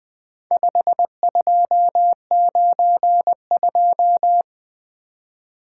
Morse code, heard as a keyed tone, about 20 wpm, 700 Hz. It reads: 5292
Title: Four Letter-Number Combinations - 20wpm